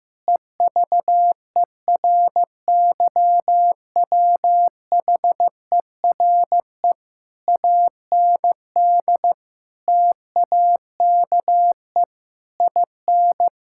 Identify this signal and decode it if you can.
Morse code, heard as a keyed tone, about 15 wpm, 700 Hz. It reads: EVERYWHERE AND TAKE IN
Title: Sentences from Top 1600 Words 15wpm